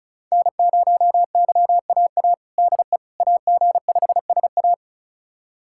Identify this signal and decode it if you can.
Morse code, heard as a keyed tone, about 36 wpm, 700 Hz. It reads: N0YAA DE AG5SA
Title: USA Callsign DE Callsign 35wpm